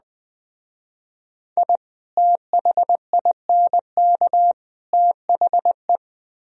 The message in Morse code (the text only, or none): I THINK THE